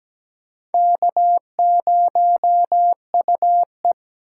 Morse code, heard as a keyed tone, about 17 wpm, 700 Hz. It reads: K0UE
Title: USA Call Signs - 4 Characters - 4x Character Spacing - 17wpm